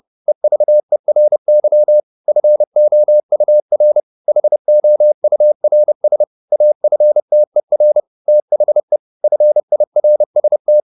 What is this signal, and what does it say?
Morse code, heard as a keyed tone, about 30 wpm, 600 Hz. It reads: EVERY FOUR HOURS AFTER THE FIRST
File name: the old man and the sea 2 30wpm